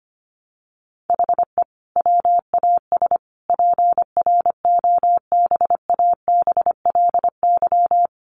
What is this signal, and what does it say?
Morse code, heard as a keyed tone, about 25 wpm, 700 Hz. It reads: HE WAS PROBABLY
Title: Sentences from Top 1300 Words 25wpm